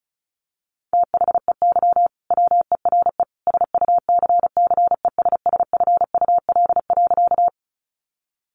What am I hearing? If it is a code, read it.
Morse code, heard as a keyed tone, about 34 wpm, 700 Hz. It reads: THEY WERE SUCCESSFUL.
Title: Sentences from Top 900 Words 35wpm